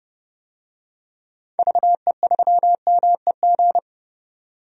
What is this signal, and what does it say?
Morse code, heard as a keyed tone, about 30 wpm, 700 Hz. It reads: VE3MEG
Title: DXCC-100 Call Signs - 6 Characters - 30wpm